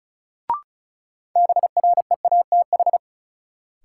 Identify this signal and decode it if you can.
Morse code, heard as a keyed tone, about 36 wpm, 700 Hz. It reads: BREATH
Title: Sets of 5 Words Encore - New Word Drill - 35wpm